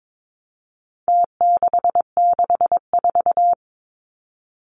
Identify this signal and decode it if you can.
Morse code, heard as a keyed tone, about 22 wpm, 700 Hz. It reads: T664
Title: Numbers - Contest Style - 4-Digit 22wpm